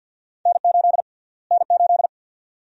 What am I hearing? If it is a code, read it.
Morse code, heard as a keyed tone, about 53 wpm, 700 Hz. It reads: N8 N8
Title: USA Call Sign Prefixes - Repeated 2x - 50wpm